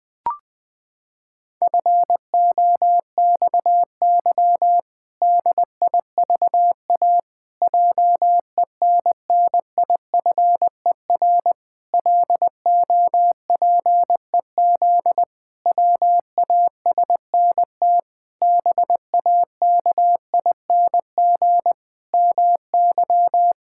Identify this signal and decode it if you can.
Morse code, heard as a keyed tone, about 20 wpm, 700 Hz. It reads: FOXY DIVA JENNIFER LOPEZ WASNT BAKING MY
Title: Pangram Challenge v2 - 20wpm